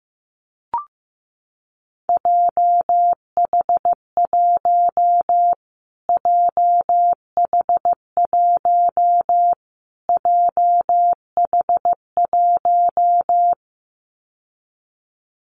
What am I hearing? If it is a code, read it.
Morse code, heard as a keyed tone, about 15 wpm, 700 Hz. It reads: JH1 JH1 JH1
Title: Callsign Prefixes for Top 25 DXCC - Repeated 3x - 15wpm